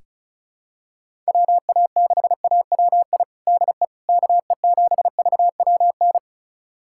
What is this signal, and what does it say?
Morse code, heard as a keyed tone, about 36 wpm, 690 Hz. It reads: WA6AWI DE KE7VWN
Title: USA Callsign DE Callsign - 6 Characters - 35wpm